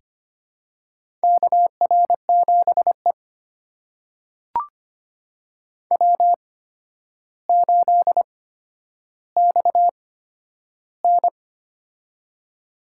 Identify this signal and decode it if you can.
Morse code, heard as a keyed tone, about 25 wpm, spaced as for 11 wpm, 700 Hz. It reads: KR7E W8XN
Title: USA Call Signs - 4 Characters - 8x Character Spacing - 25wpm